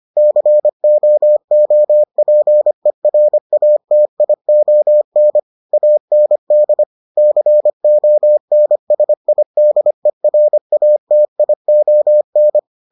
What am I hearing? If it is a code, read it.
Morse code, heard as a keyed tone, about 25 wpm, 600 Hz. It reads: COOPERATION AND CONSIDERATION